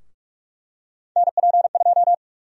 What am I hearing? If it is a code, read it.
Morse code, heard as a keyed tone, about 46 wpm, 700 Hz. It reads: NP2
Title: USA Call Sign Prefixes - Repeated 2x - 45wpm